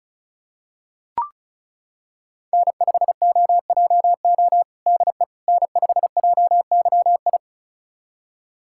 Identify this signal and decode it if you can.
Morse code, heard as a keyed tone, about 36 wpm, 700 Hz. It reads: N5OJO DE N5JYI
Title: USA Callsign DE Callsign - 5 Characters - 35wpm